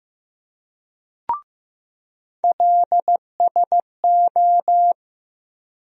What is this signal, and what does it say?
Morse code, heard as a keyed tone, about 15 wpm, 700 Hz. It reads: LSO
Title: USA Call Sign Postfixes - 3 Letters 15wpm